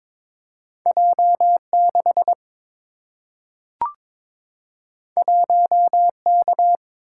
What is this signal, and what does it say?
Morse code, heard as a keyed tone, about 23 wpm, 700 Hz. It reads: J 6 1 K
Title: Single-Letter Single-Number 22wpm